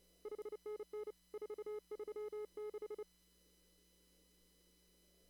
Morse code, heard as a keyed tone, about 30 wpm, 420 Hz. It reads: 5NN 436
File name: CW / kt5j wpx cw-op-k5pi